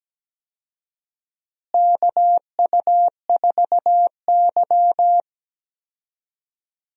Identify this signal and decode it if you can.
Morse code, heard as a keyed tone, about 17 wpm, 700 Hz. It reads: KU4Y
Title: USA Call Signs - 4 Characters - Repeated 2x - 17wpm